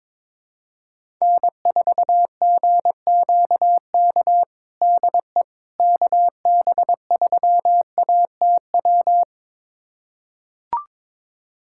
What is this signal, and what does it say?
Morse code, heard as a keyed tone, about 22 wpm, 700 Hz. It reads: N4GQK DE KB3ATW
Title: USA Callsign DE Callsign 22wpm